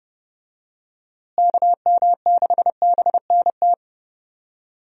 Morse code, heard as a keyed tone, about 30 wpm, 700 Hz. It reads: KM6BNT
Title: USA Call Signs - 6 Characters - Repeated 2x - 30wpm